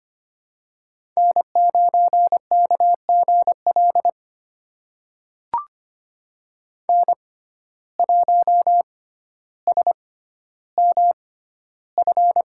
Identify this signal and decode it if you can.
Morse code, heard as a keyed tone, about 25 wpm, spaced as for 13 wpm, 700 Hz. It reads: N9KGL N1SMF
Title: USA Call Signs - 6x Character Spacing - 25wpm